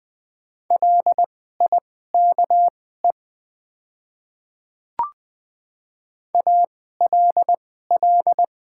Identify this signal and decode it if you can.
Morse code, heard as a keyed tone, about 20 wpm, spaced as for 15 wpm, 700 Hz. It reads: LIKE ALL
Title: 2x Character Spacing - Top 100 Words - 20wpm